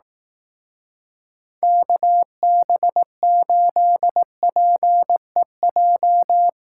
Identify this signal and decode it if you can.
Morse code, heard as a keyed tone, about 18 wpm, 700 Hz. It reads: KB8PEJ
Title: Speed Racing - USA 2x3 Call Signs 15wpm